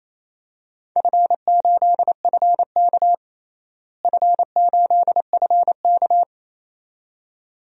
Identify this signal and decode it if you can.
Morse code, heard as a keyed tone, about 28 wpm, 700 Hz. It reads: F8FK F8FK
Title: DXCC-100 Call Signs - 4 Characters - Repeated 2x - 28wpm